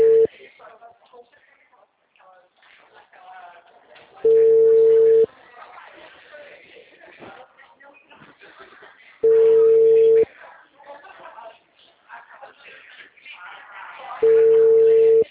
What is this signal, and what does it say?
Ringback tone (China, 450 Hz), with speech.